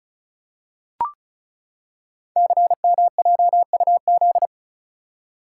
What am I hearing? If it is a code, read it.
Morse code, heard as a keyed tone, about 36 wpm, 690 Hz. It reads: CMJUZ
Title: Code Groups - 5 Letters - 35wpm